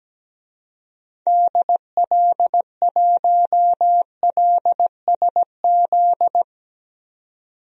Morse code, heard as a keyed tone, about 17 wpm, 700 Hz. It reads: DL1LSZ
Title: DXCC-100 Call Signs - 6 Characters - 17wpm